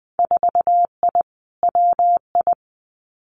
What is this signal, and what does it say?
Morse code, heard as a keyed tone, about 20 wpm, 700 Hz. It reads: 4I WI
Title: Winter Field Day Exchanges 20wpm